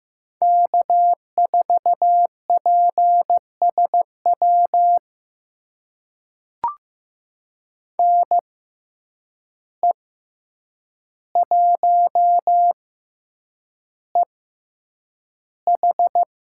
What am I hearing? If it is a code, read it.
Morse code, heard as a keyed tone, about 15 wpm, spaced as for 8 wpm, 700 Hz. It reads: K4PSW NE1EH
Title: USA Call Signs - 5 Characters - 6x Character Spacing - 15wpm